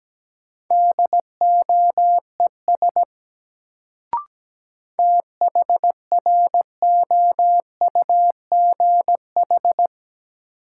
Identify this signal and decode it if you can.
Morse code, heard as a keyed tone, about 17 wpm, 700 Hz. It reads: DOES THROUGH
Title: Sentences from Top 200 Words - New Word Drill - 17wpm